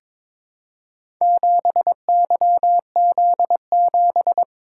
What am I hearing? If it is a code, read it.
Morse code, heard as a keyed tone, about 22 wpm, 700 Hz. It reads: 7YZ7
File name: practice / 22wpm / Four Letter-Number Combinations - 22wpm